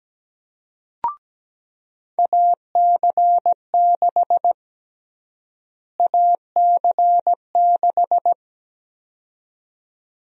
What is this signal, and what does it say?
Morse code, heard as a keyed tone, about 17 wpm, 700 Hz. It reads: AC6 AC6
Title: USA Call Sign Prefixes - Repeated 2x - 17wpm